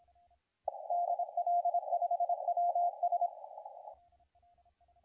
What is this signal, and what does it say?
Morse code, heard as a keyed tone, about 24 wpm, 720 Hz, with radio noise. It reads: DL5WS
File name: la4c / sac-cw-2017 / CONTEST AUDIO 2100Z